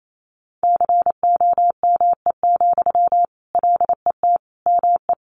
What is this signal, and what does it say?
Morse code, heard as a keyed tone, about 28 wpm, 700 Hz. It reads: COME, LET ME
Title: Sentences from Top 500 Words 28wpm